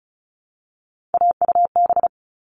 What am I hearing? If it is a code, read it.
Morse code, heard as a keyed tone, about 34 wpm, 700 Hz. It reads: AUB
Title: All Cut Numbers - 3-Digit 35wpm